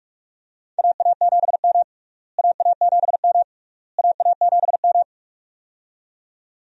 Morse code, heard as a keyed tone, about 42 wpm, 700 Hz. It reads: AA7M AA7M AA7M
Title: USA Call Signs - 4 Characters - Repeated 3x - 45wpm